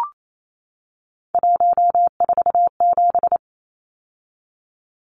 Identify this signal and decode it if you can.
Morse code, heard as a keyed tone, about 28 wpm, 700 Hz. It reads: 147
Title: Numbers - 3-Digit 28wpm